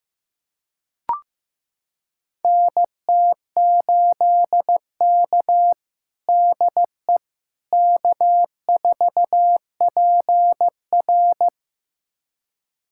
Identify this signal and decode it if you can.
Morse code, heard as a keyed tone, about 15 wpm, 700 Hz. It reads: NT8K DE K4PR
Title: USA Callsign DE Callsign - 4 Characters - 15wpm